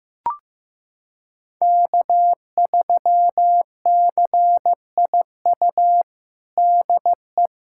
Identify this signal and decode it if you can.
Morse code, heard as a keyed tone, about 15 wpm, 700 Hz. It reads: K3CIU DE
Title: USA Callsign DE Callsign - 5 Characters - 15wpm